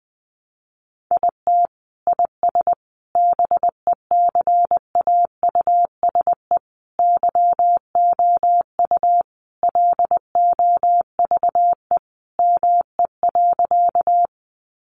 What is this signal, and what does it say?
Morse code, heard as a keyed tone, about 20 wpm, 700 Hz. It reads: IT IS BECAUSE YOU LOVE ME.
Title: Sentences from Top 600 Words 20wpm